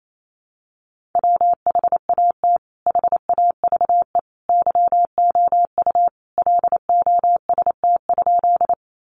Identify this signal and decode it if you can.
Morse code, heard as a keyed tone, about 28 wpm, 700 Hz. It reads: WHAT HAVE YOU LOST?
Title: Sentences from Top 1400 Words 28wpm